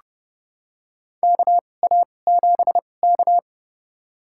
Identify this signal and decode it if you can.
Morse code, heard as a keyed tone, about 29 wpm, spaced as for 21 wpm, 700 Hz. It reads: KA7K
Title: USA Call Signs - 4 Characters - 2x Character Spacing - 30wpm